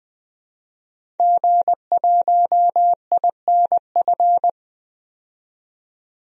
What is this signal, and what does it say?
Morse code, heard as a keyed tone, about 21 wpm, 700 Hz. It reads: G1INF
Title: DXCC-100 Call Signs - 5 Characters - 20wpm